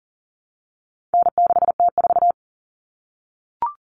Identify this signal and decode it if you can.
Morse code, heard as a keyed tone, about 40 wpm, 700 Hz. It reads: N6T4
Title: All Cut Numbers - 4-Digit 40wpm